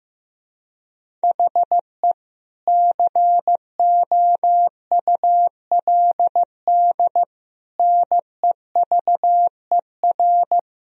Morse code, heard as a keyed tone, about 15 wpm, 700 Hz. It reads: HE COULD NEVER
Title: Sentences from Top 1000 Words - Review Part 3 - 15wpm